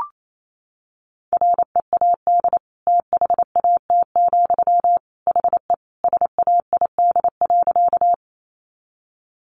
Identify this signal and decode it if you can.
Morse code, heard as a keyed tone, about 28 wpm, 700 Hz. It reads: READ THAT, HE SAID.
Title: Sentences from Top 200 Words 28wpm